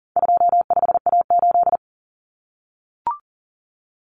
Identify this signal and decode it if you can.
Morse code, heard as a keyed tone, about 40 wpm, 700 Hz. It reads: 25A8
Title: Common Cut Numbers - 4-Digit 40wpm